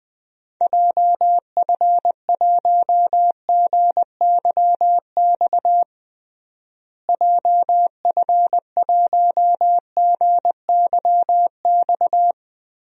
Morse code, heard as a keyed tone, about 20 wpm, 700 Hz. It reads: JF1GYX JF1GYX
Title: DXCC-100 Call Signs - 6 Characters - Repeated 3x - 20wpm